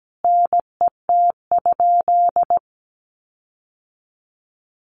Morse code, heard as a keyed tone, about 17 wpm, 700 Hz. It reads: NET?
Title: Sentences from Top 1400 Words 17wpm